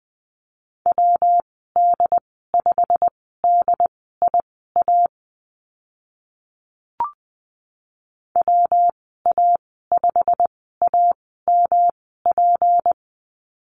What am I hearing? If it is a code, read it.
Morse code, heard as a keyed tone, about 20 wpm, spaced as for 14 wpm, 700 Hz. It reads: WD5DIA WA5AMP
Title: 167 Lesson 27g - Number 7 - Call Signs 20wpm